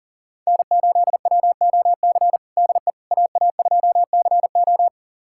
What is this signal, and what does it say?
Morse code, heard as a keyed tone, about 40 wpm, 700 Hz. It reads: N8WOC DE AA2CO
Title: USA Callsign DE Callsign - 5 Characters - 40wpm